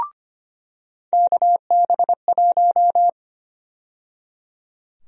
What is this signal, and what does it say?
Morse code, heard as a keyed tone, about 25 wpm, 700 Hz. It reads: KB1
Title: USA Call Sign Prefixes - 25wpm